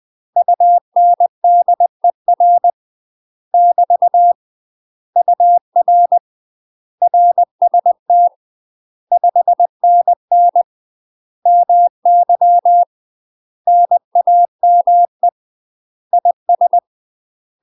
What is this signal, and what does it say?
Morse code, heard as a keyed tone, about 20 wpm, 700 Hz. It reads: UNDER <BT> UR RST 5NN MY NAME IS
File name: UK MASTER QSOs TRAINING / 20WPM / UK MASTER QSOs 0042